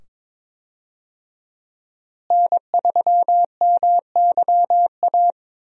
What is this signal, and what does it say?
Morse code, heard as a keyed tone, about 22 wpm, 700 Hz. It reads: N3MYA
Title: USA Call Signs - 5 Characters - Repeated 2x - 22wpm